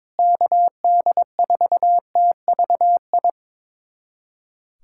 Morse code, heard as a keyed tone, about 23 wpm, 700 Hz. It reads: KD4TVI
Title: USA Callsign DE Callsign 22wpm